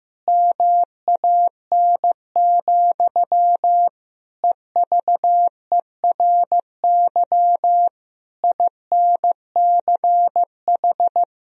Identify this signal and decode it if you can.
Morse code, heard as a keyed tone, about 15 wpm, 700 Hz. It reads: MAN, EVERY INCH
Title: Sentences from Top 1000 Words - Review Part 3 - 15wpm